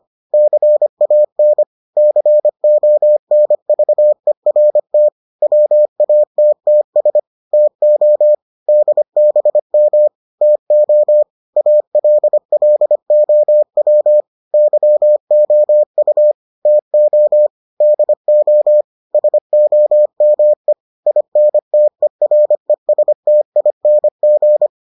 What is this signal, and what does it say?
Morse code, heard as a keyed tone, about 25 wpm, 600 Hz. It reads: CAN CONVERT WATTS TO DBM TO ALLOW YOU TO DO SOME INTERESTING